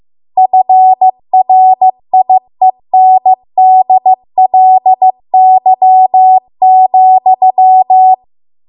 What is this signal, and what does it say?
Morse code, heard as a keyed tone, about 15 wpm, 750 Hz. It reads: FRIENDLY,
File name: Archive / 15 WPM / 15WPM